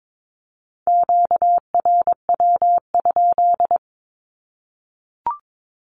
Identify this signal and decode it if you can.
Morse code, heard as a keyed tone, about 22 wpm, 700 Hz. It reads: QRW?
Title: Q-Codes Repeated 3x 22wpm